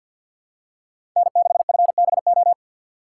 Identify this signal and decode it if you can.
Morse code, heard as a keyed tone, about 49 wpm, 700 Hz. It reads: N6FBO
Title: USA Call Signs - 5 Characters - 6x Character Spacing - 50wpm